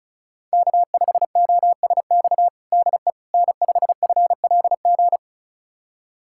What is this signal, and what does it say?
Morse code, heard as a keyed tone, about 34 wpm, 700 Hz. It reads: K5OSX DE N5FLG
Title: USA Callsign DE Callsign - 5 Characters - 35wpm